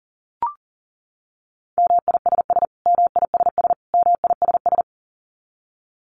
Morse code, heard as a keyed tone, about 40 wpm, 700 Hz. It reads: MISS MISS MISS
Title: Top 500 Words Repeated 40wpm